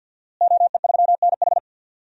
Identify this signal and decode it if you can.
Morse code, heard as a keyed tone, about 50 wpm, 700 Hz. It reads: OE3NH
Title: DXCC-100 Call Signs - 5 Characters - Repeated 2x - 50wpm